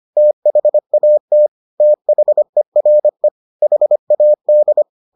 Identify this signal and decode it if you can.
Morse code, heard as a keyed tone, about 25 wpm, 600 Hz. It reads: THAT THERE HAD